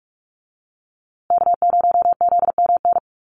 Morse code, heard as a keyed tone, about 44 wpm, 700 Hz. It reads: K0ZMN
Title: USA Call Signs Repeated 2x 45wpm